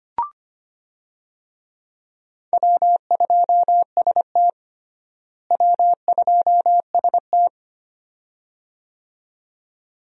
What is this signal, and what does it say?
Morse code, heard as a keyed tone, about 25 wpm, 700 Hz. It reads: W2ST W2ST
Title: USA Call Signs - 4 Characters - Repeated 2x - 25wpm